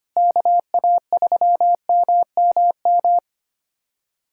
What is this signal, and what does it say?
Morse code, heard as a keyed tone, about 25 wpm, 700 Hz. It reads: KA3MMM
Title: USA Callsign DE Callsign 25wpm